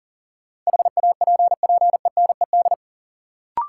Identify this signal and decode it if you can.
Morse code, heard as a keyed tone, about 40 wpm, 700 Hz. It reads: HAPPENED
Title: Sentences from Top 1000 Words - Review Part 2 - New Word Drill - 40wpm